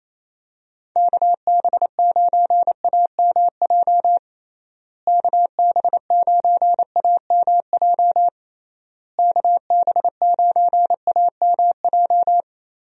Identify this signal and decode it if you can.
Morse code, heard as a keyed tone, about 28 wpm, 700 Hz. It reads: KB9AMJ KB9AMJ KB9AMJ
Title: USA Call Signs - 6 Characters - Repeated 3x - 28wpm